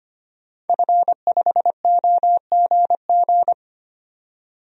Morse code, heard as a keyed tone, about 25 wpm, 700 Hz. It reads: F5OGG
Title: DXCC-100 Call Signs - 5 Characters - 25wpm